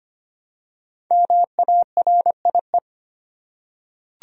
Morse code, heard as a keyed tone, about 25 wpm, 700 Hz. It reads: MARIE
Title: USA Common Names - 25wpm